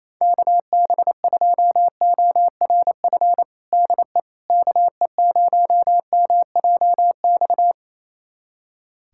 Morse code, heard as a keyed tone, about 28 wpm, 700 Hz. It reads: KB2ORF DE KE0MJX
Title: USA Callsign DE Callsign - 6 Characters - 28wpm